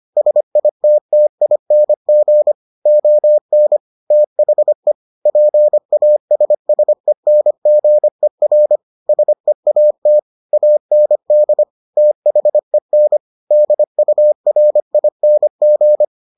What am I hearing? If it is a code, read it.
Morse code, heard as a keyed tone, about 25 wpm, 600 Hz. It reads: SITTING ON THE PASSENGER SEAT AND THEN DURING